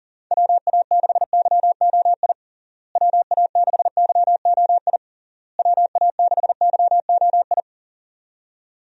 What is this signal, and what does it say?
Morse code, heard as a keyed tone, about 37 wpm, 700 Hz. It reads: WA6YOI WA6YOI WA6YOI
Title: USA Call Signs - 6 Characters - Repeated 3x - 40wpm